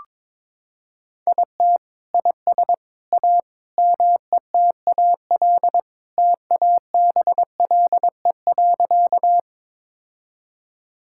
Morse code, heard as a keyed tone, about 22 wpm, 700 Hz. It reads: IT IS A METAL TABLE.
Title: Sentences from Top 1000 Words - Review Part 6 - 22wpm